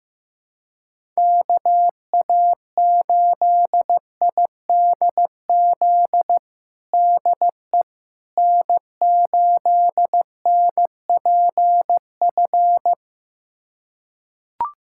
Morse code, heard as a keyed tone, about 16 wpm, 700 Hz. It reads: KA8IDZ DE N8NPF
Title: USA Callsign DE Callsign 15wpm